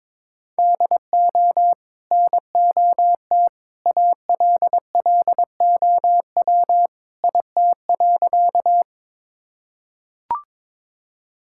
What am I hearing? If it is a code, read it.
Morse code, heard as a keyed tone, about 22 wpm, 700 Hz. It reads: DO NOT ALLOW IT.
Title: Sentences from Top 1000 Words - Review Part 5 - 22wpm